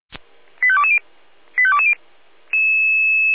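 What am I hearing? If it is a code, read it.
Five-tone selective call (ZVEI), heard as quick runs of five tones, tones of about 70 ms. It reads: 85220 85220
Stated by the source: ZVEI five-tone call 85232 received by SDR.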